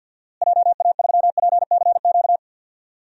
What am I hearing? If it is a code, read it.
Morse code, heard as a keyed tone, about 46 wpm, 700 Hz. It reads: JA3PCQ
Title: DXCC-100 Call Signs - 6 Characters - Repeated 3x - 50wpm